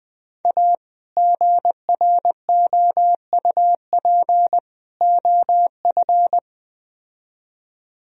Morse code, heard as a keyed tone, about 20 wpm, 700 Hz. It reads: A GROUP OF